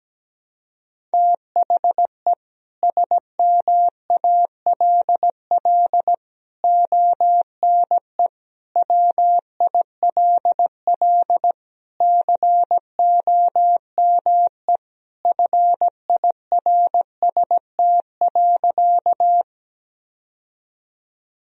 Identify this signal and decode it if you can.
Morse code, heard as a keyed tone, about 17 wpm, 700 Hz. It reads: THE SMALL ONE WILL COME FIRST.
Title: Sentences from Top 1000 Words - Review Part 1 - 17wpm